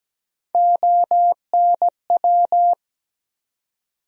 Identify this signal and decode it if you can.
Morse code, heard as a keyed tone, about 17 wpm, 700 Hz. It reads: ONW
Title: USA Call Sign Postfixes - 3 Letters 17wpm